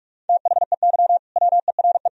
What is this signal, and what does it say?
Morse code, heard as a keyed tone, about 46 wpm, 700 Hz. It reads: THEY WERE
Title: Sentences from Top 1000 Words - Review Part 3 - 45wpm